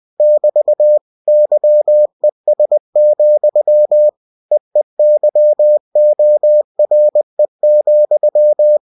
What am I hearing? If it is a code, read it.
Morse code, heard as a keyed tone, about 20 wpm, 600 Hz. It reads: <BT> YES, EEYORE,
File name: cw-books / winnie-the-pooh / winnie-20wpm-page-0054